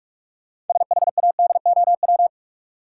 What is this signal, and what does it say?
Morse code, heard as a keyed tone, about 46 wpm, 700 Hz. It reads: SHADOW
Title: Sets of 5 Words Encore - New Word Drill - 45wpm